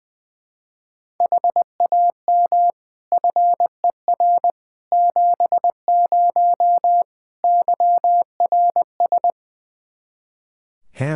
Morse code, heard as a keyed tone, about 20 wpm, 700 Hz. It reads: HAM FER 70 YRS
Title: Practice QSO - Exchange 3 - More Info - 20wpm